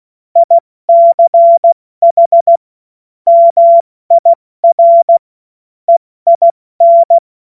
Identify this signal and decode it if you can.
Morse code, heard as a keyed tone, about 16 wpm, 680 Hz. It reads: ICH MIR EIN
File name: Morsekurs / Witze / Witz 22. Februar 2022 70 F80